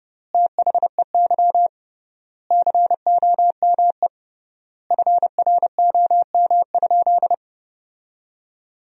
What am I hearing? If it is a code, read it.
Morse code, heard as a keyed tone, about 30 wpm, 700 Hz. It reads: THEY COME FROM?
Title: 3x Word-Spacing - Sentences from Top 200 Words - 30wpm